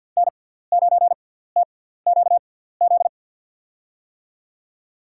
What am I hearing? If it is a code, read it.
Morse code, heard as a keyed tone, about 53 wpm, spaced as for 17 wpm, 700 Hz. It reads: N9TQZ
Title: USA Call Signs - 5 Characters - 6x Character Spacing - 50wpm